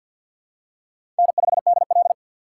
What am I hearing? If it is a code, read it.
Morse code, heard as a keyed tone, about 50 wpm, 700 Hz. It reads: N5DL
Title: USA Call Signs - 4 Characters - 3x Character Spacing - 50wpm